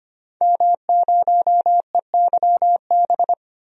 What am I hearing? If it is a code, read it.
Morse code, heard as a keyed tone, about 25 wpm, 700 Hz. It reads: M0EYB
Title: DXCC-100 Call Signs - 5 Characters - 25wpm